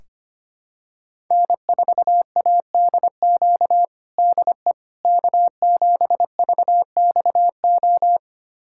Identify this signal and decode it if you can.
Morse code, heard as a keyed tone, about 25 wpm, 700 Hz. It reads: N4ADQ DE K7VXO
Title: USA Callsign DE Callsign - 5 Characters - 25wpm